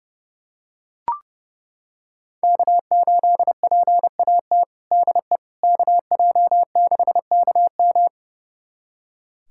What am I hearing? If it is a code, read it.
Morse code, heard as a keyed tone, about 30 wpm, 700 Hz. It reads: K8PAT DE KJ6KM
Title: USA Callsign DE Callsign - 5 Characters - 30wpm